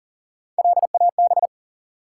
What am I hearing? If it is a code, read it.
Morse code, heard as a keyed tone, about 42 wpm, 700 Hz. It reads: LAB
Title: Three Letter Words - 40wpm